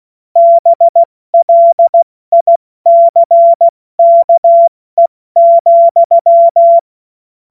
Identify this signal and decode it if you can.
Morse code, heard as a keyed tone, about 16 wpm, 680 Hz. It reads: BLICKE,
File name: Morsekurs / Witze / Witz 18. Januar 2022, 70 F80